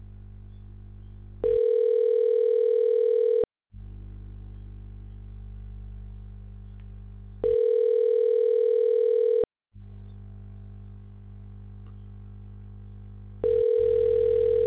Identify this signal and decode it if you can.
Ringback tone (North America), 440 + 480 Hz.